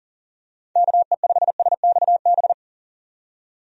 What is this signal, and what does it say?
Morse code, heard as a keyed tone, about 39 wpm, 700 Hz. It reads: KE5SXB
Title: USA Call Signs - 6 Characters - 40wpm